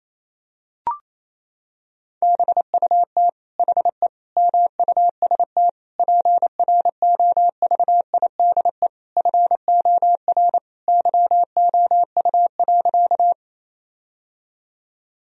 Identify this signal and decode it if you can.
Morse code, heard as a keyed tone, about 27 wpm, 700 Hz. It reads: BUT HE MUST PROVIDE FOR YOU.
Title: Sentences from Top 1000 Words - Review Part 3 - 28wpm